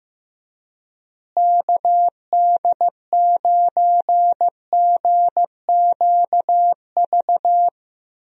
Morse code, heard as a keyed tone, about 15 wpm, 700 Hz. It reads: KD9GQV
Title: USA Call Signs - 6 Characters - Repeated 3x - 15wpm